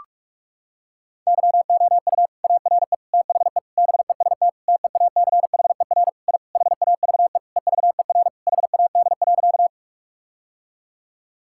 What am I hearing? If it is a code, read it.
Morse code, heard as a keyed tone, about 41 wpm, 700 Hz. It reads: YOU ARE THE BEST TEACHER I HAVE EVER HAD.
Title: Sentences from Top 1000 Words - Review Part 6 - 45wpm